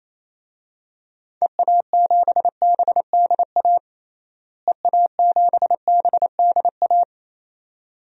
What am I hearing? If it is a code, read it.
Morse code, heard as a keyed tone, about 28 wpm, 700 Hz. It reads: EA7BDA EA7BDA
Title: DXCC-100 Call Signs - 6 Characters - Repeated 2x - 28wpm